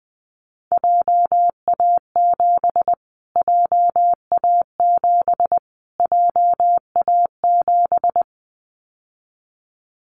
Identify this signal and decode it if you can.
Morse code, heard as a keyed tone, about 19 wpm, 700 Hz. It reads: JA7 JA7 JA7
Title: Callsign Prefixes for Top 25 DXCC - Repeated 3x - 20wpm